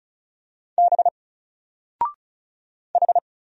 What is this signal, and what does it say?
Morse code, heard as a keyed tone, about 33 wpm, 700 Hz. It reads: B H
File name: Single-Letter-Number-35wpm